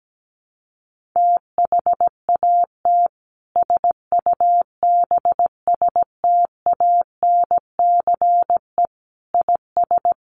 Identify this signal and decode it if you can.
Morse code, heard as a keyed tone, about 17 wpm, 700 Hz. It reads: THAT SUBSTANCE IS
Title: Sentences from Top 1500 Words 17wpm